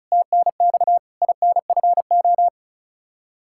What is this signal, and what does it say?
Morse code, heard as a keyed tone, about 36 wpm, 700 Hz. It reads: TNX INFO
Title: Practice QSO - Exchange 3 - More Info - 35wpm